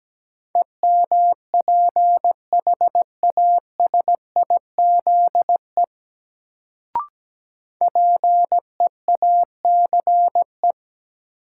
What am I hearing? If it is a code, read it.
Morse code, heard as a keyed tone, about 17 wpm, 700 Hz. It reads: EMPHASIZE PEACE
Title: Sentences from Top 1500 Words - New Word Drill - 17wpm